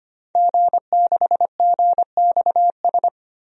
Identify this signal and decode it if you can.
Morse code, heard as a keyed tone, about 25 wpm, 700 Hz. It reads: G6GXS
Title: DXCC-100 Call Signs - 5 Characters - Repeated 2x - 25wpm